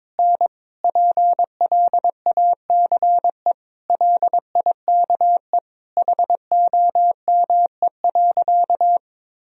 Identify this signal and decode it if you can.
Morse code, heard as a keyed tone, about 22 wpm, 700 Hz. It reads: N PLACE LIKE HOME.
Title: Sentences from Top 200 Words and Abbreviations 22wpm